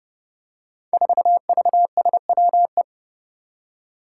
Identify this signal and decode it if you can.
Morse code, heard as a keyed tone, about 31 wpm, 700 Hz. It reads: 4VSWE
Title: Five Challenging Letter-Number Combinations 30wpm